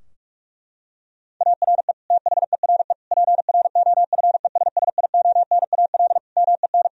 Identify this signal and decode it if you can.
Morse code, heard as a keyed tone, about 41 wpm, 700 Hz. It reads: ARE THERE PROFESSIONAL MEN
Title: Sentences from Top 1000 Words - Review Part 4 - 45wpm